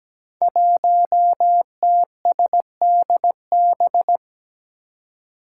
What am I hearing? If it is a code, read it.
Morse code, heard as a keyed tone, about 17 wpm, 700 Hz. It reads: 1TSDB
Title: Five Challenging Letter-Number Combinations 17wpm